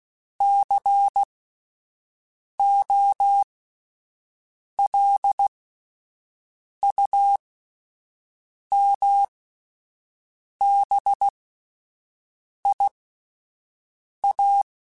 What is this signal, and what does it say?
Morse code, heard as a keyed tone, about 16 wpm, spaced as for 5 wpm, 800 Hz. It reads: COLUMBIA